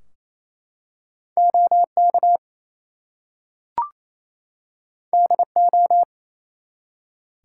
Morse code, heard as a keyed tone, about 29 wpm, 700 Hz. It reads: OK DO
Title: Two Letter Words - 28wpm